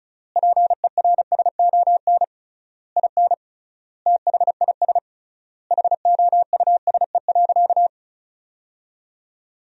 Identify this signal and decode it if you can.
Morse code, heard as a keyed tone, about 34 wpm, 700 Hz. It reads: PERSON IN THIS HOUSE.
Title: 3x Word-Spacing - Sentences from Top 400 Words - 35wpm